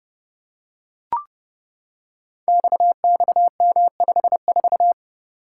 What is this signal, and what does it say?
Morse code, heard as a keyed tone, about 30 wpm, 700 Hz. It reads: XXM54
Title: Code Groups - 5 Letter:Numbers - 30wpm